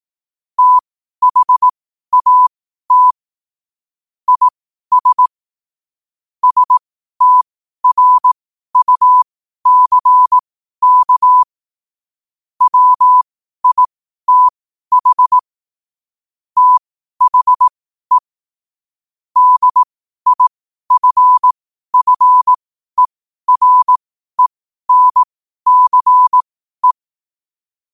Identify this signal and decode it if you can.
Morse code, heard as a keyed tone, about 18 wpm, spaced as for 12 wpm, 1000 Hz. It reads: THAT IS STRUCK WITH THE DIFFERENCE
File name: qotdfiles / 12wpm